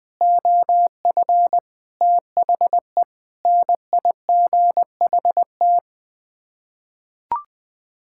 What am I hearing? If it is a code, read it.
Morse code, heard as a keyed tone, about 20 wpm, 700 Hz. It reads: OF THE NIGHT
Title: Sets of 6 Words - 20wpm